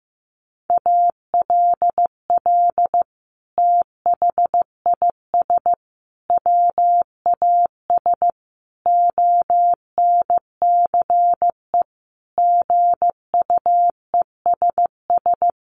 Morse code, heard as a keyed tone, about 15 wpm, 700 Hz. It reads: ALL THIS WAS ONCE GUESS
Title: Sentences from Top 1500 Words 15wpm